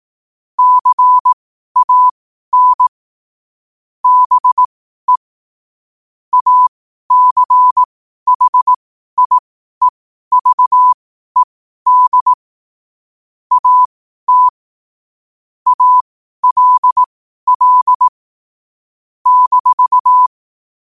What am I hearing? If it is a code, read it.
Morse code, heard as a keyed tone, about 18 wpm, spaced as for 12 wpm, 1000 Hz. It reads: CAN BE ACHIEVED AT ALL -
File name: qotdfiles / 12wpm